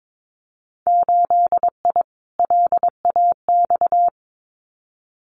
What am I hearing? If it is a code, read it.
Morse code, heard as a keyed tone, about 22 wpm, 700 Hz. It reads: 8I LAX
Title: Winter Field Day Exchanges Encore 22wpm